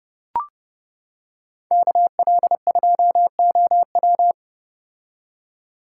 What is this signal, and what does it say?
Morse code, heard as a keyed tone, about 30 wpm, 700 Hz. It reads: KL2OW